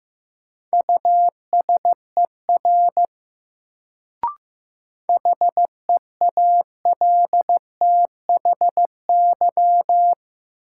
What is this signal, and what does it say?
Morse code, heard as a keyed tone, about 15 wpm, 700 Hz. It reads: USER HEALTHY
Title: Sentences from Top 1000 Words - Review Part 5 - New Word Drill - 15wpm